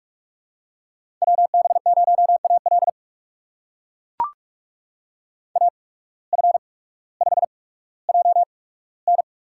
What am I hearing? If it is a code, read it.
Morse code, heard as a keyed tone, about 44 wpm, spaced as for 19 wpm, 700 Hz. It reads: WB0AL AF5JN
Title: USA Call Signs - 5 Characters - 8x Character Spacing - 45wpm